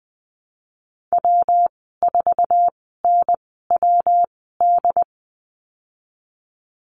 Morse code, heard as a keyed tone, about 19 wpm, spaced as for 14 wpm, 700 Hz. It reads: W4NWD
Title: 167 Lesson 27g - Number 7 - Call Signs 20wpm